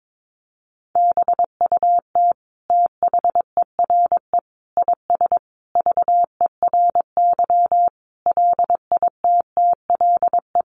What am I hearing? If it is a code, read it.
Morse code, heard as a keyed tone, about 22 wpm, 700 Hz. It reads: BUT THERE IS VERY LITTLE
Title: Sentences from Top 1000 Words 22wpm